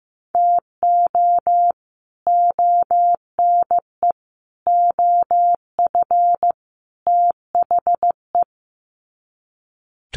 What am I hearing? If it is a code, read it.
Morse code, heard as a keyed tone, about 15 wpm, 700 Hz. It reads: TO ONE OF THE
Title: Sets of 4 Words 15wpm